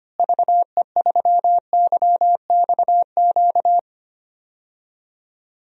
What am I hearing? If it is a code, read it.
Morse code, heard as a keyed tone, about 25 wpm, 700 Hz. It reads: VE3YXQ
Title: DXCC-100 Call Signs - 6 Characters - Repeated 3x - 25wpm